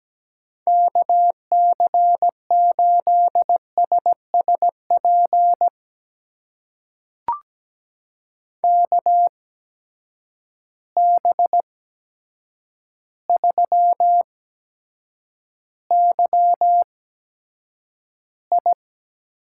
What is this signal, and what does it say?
Morse code, heard as a keyed tone, about 17 wpm, spaced as for 8 wpm, 700 Hz. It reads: KC8SSP KB3YI
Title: USA Call Signs - 8x Character Spacing - 17wpm